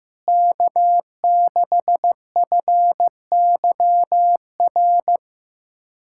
Morse code, heard as a keyed tone, about 15 wpm, 700 Hz. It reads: K6FYR
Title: USA Callsign DE Callsign - 5 Characters - 15wpm